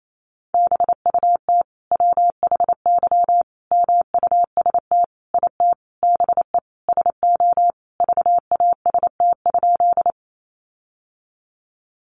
Morse code, heard as a keyed tone, about 28 wpm, 700 Hz. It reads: BUT WHY MUST IT BE SO VAST?
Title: Sentences from Top 1300 Words 28wpm